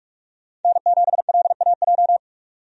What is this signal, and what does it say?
Morse code, heard as a keyed tone, about 43 wpm, 700 Hz. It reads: N7LAJ
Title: USA Call Signs - 5 Characters - 45wpm